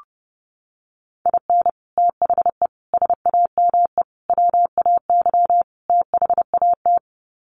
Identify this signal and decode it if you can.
Morse code, heard as a keyed tone, about 30 wpm, 700 Hz. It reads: IN THE SAME WAY THAT
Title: Sets of 5 Words 30wpm